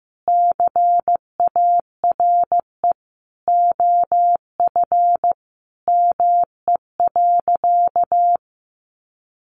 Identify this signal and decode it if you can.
Morse code, heard as a keyed tone, about 15 wpm, 700 Hz. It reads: CARE OF ME.
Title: Sentences from Top 300 Words 15wpm